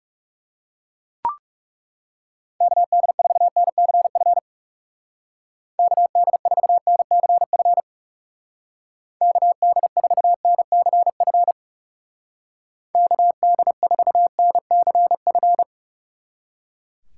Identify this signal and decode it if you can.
Morse code, heard as a keyed tone, about 38 wpm, 700 Hz. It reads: KD4NCF KD4NCF KD4NCF KD4NCF
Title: Speed Racing - USA 2x3 Call Signs 30wpm